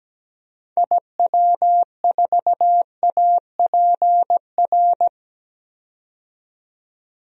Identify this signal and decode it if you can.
Morse code, heard as a keyed tone, about 17 wpm, 700 Hz. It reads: IW4APR
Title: DXCC-100 Call Signs - 6 Characters - 17wpm